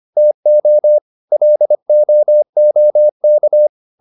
Morse code, heard as a keyed tone, about 25 wpm, 600 Hz. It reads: TO LOOK